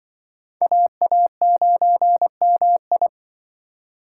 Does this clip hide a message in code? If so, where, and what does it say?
Yes: Morse code from 0.6 to 3.1 s, heard as a keyed tone, about 25 wpm, 700 Hz. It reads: AA9MI